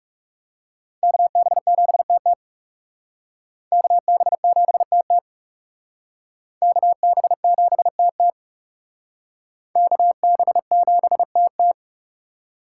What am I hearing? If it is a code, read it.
Morse code, heard as a keyed tone, about 37 wpm, 700 Hz. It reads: KB7TT KB7TT KB7TT KB7TT
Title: Speed Racing - USA 2x2 Call Signs 30wpm